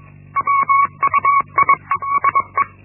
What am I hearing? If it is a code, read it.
Morse code, heard as a keyed tone, about 21 wpm, 1120 Hz. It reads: WUIESE
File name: Linksmakalnis-1